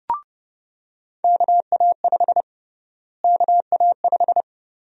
Morse code, heard as a keyed tone, about 30 wpm, 700 Hz. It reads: KA5 KA5
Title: USA Call Sign Prefixes - Repeated 2x - 30wpm